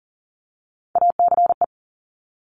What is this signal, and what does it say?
Morse code, heard as a keyed tone, about 42 wpm, 700 Hz. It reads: ACE
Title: Top Three Letter Combinations in Words 40wpm